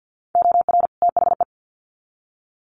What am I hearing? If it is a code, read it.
Morse code, heard as a keyed tone, about 50 wpm, 700 Hz. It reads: OR THE
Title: Sets of 2 Words 50wpm